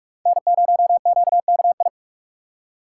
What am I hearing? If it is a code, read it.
Morse code, heard as a keyed tone, about 44 wpm, 700 Hz. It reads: N0QKI